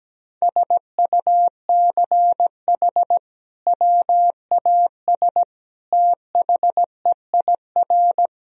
Morse code, heard as a keyed tone, about 17 wpm, 700 Hz. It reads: SUCH WAS THEIR
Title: Sentences from Top 1000 Words - Review Part 2 - 17wpm